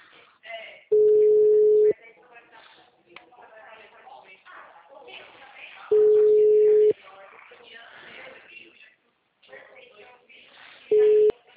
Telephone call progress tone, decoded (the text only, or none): ringback tone